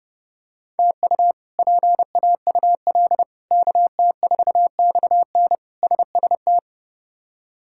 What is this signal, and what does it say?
Morse code, heard as a keyed tone, about 32 wpm, 700 Hz. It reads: TU PAUL KT4XN SST
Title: K1USN SST QSO Practice - 30wpm